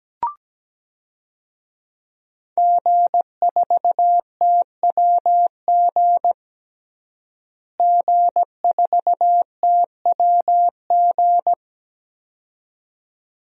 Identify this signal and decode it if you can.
Morse code, heard as a keyed tone, about 17 wpm, 700 Hz. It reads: G4TWG G4TWG
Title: DXCC-100 Call Signs - 5 Characters - Repeated 2x - 17wpm